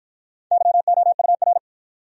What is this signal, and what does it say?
Morse code, heard as a keyed tone, about 52 wpm, 700 Hz. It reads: YOUR